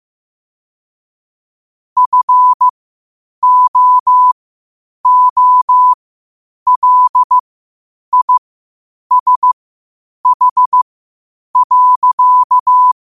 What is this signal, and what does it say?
Morse code, heard as a keyed tone, about 15 wpm, spaced as for 8 wpm, 1000 Hz. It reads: FOOLISH.
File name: 8wpm